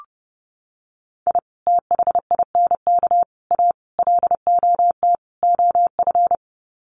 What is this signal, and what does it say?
Morse code, heard as a keyed tone, about 30 wpm, 700 Hz. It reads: I THINK A LOT OF
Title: Sets of 5 Words 30wpm